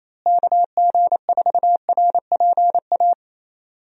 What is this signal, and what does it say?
Morse code, heard as a keyed tone, about 27 wpm, 700 Hz. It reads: KG4RPA
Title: USA Call Signs - 6 Characters - Repeated 3x - 28wpm